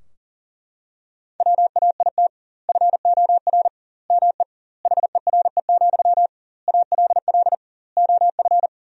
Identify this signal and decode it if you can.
Morse code, heard as a keyed tone, about 37 wpm, 700 Hz. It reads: WAIT FOR ME HERE, ALL OF
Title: 2x Word-Spacing - Sentences from Top 500 Words - 40wpm